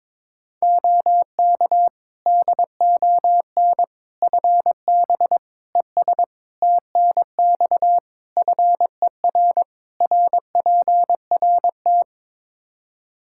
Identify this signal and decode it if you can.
Morse code, heard as a keyed tone, about 22 wpm, 700 Hz. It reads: OK DON FB ES TNX FER RPRT
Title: Practice QSO - Exchange 2 - Info - 22wpm